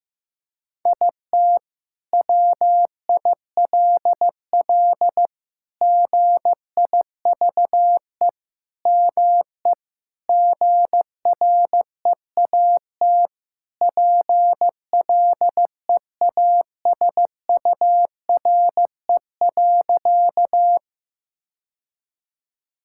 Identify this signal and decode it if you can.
Morse code, heard as a keyed tone, about 16 wpm, 700 Hz. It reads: IT WILL GIVE ME GREAT PLEASURE.
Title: Sentences from Top 1600 Words 15wpm